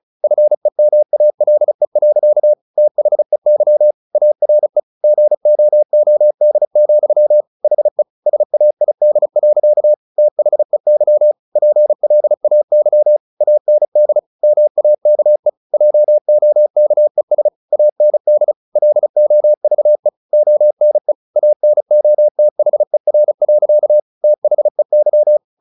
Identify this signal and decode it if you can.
Morse code, heard as a keyed tone, about 35 wpm, 600 Hz. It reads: FEMALE. THEY ARE GOOD, HE SAID. THEY PLAY AND MAKE JOKES AND LOVE ONE ANOTHER. THEY
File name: the old man and the sea / the old man and the sea 1 35wpm